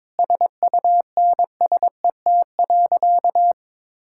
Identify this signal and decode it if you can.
Morse code, heard as a keyed tone, about 22 wpm, 700 Hz. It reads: SUNSET.
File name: Sentences with 7-Word Phrases - 22wpm